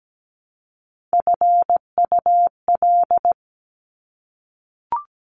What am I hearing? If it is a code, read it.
Morse code, heard as a keyed tone, about 17 wpm, 700 Hz. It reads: FUL
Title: Top Three Letter Combinations in Words 17wpm